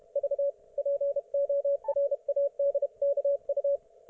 Morse code, heard as a keyed tone, about 32 wpm, 560 Hz, with radio noise. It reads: V PORADKU